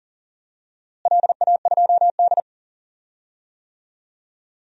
Morse code, heard as a keyed tone, about 40 wpm, 700 Hz. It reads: LA2D
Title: DXCC-100 Call Signs - 4 Characters - Repeated 3x - 40wpm